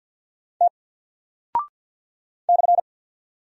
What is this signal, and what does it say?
Morse code, heard as a keyed tone, about 48 wpm, 700 Hz. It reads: T /
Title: Single Character (Letter:Number:Punctuation:Prosign) Alt 50wpm